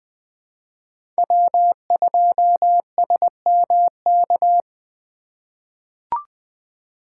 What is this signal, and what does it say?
Morse code, heard as a keyed tone, about 20 wpm, 700 Hz. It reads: W2SMK
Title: USA Call Signs - 5 Characters - 6x Character Spacing - 20wpm